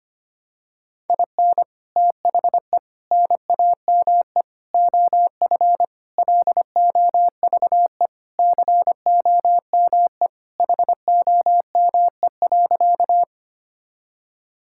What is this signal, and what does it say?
Morse code, heard as a keyed tone, about 25 wpm, 700 Hz. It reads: IN THE NAME OF LOVE COME HOME.
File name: practice / 25wpm / Sentences with 3-Word Phrases - 25wpm